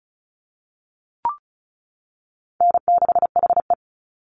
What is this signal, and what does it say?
Morse code, heard as a keyed tone, about 34 wpm, 700 Hz. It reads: N6HE
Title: Call Signs - 35wpm